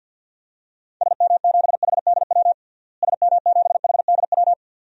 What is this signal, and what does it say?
Morse code, heard as a keyed tone, about 51 wpm, 700 Hz. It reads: SM7HDW SM7HDW
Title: DXCC-100 Call Signs - 6 Characters - Repeated 2x - 50wpm